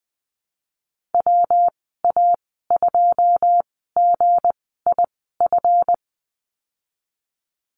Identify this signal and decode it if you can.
Morse code, heard as a keyed tone, about 20 wpm, spaced as for 15 wpm, 700 Hz. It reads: WA2GIF
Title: 179 Lesson 29g - Slash - Call Signs 20wpm